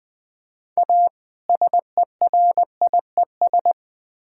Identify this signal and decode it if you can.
Morse code, heard as a keyed tone, about 20 wpm, 700 Hz. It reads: A SERIES
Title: Build A Phrase - 3-Word Phrases - 20wpm